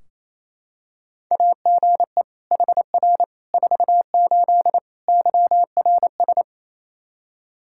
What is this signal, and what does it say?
Morse code, heard as a keyed tone, about 28 wpm, 700 Hz. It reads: AGE HR 48 YRS
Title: K1USN SST QSO Practice 28wpm